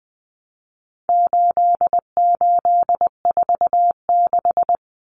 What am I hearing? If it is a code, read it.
Morse code, heard as a keyed tone, about 20 wpm, 700 Hz. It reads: 8846
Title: Numbers - 4-Digit 20wpm